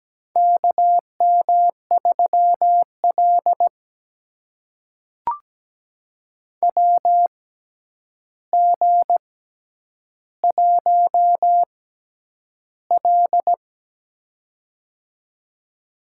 Morse code, heard as a keyed tone, about 17 wpm, spaced as for 9 wpm, 700 Hz. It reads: KM3L WG1L
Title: USA Call Signs - 4 Characters - 6x Character Spacing - 17wpm